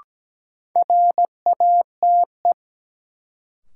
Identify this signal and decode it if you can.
Morse code, heard as a keyed tone, about 17 wpm, 700 Hz. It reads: RATE
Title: Sentences from Top 700 Words - New Word Drill - 17wpm